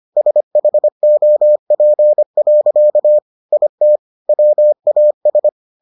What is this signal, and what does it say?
Morse code, heard as a keyed tone, about 25 wpm, 600 Hz. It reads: SHOP. IT WAS